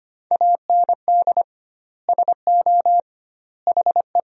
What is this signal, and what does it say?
Morse code, heard as a keyed tone, about 25 wpm, 700 Hz. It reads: AND SO HE
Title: 2x Word-Spacing - Sentences from Top 500 Words - 25wpm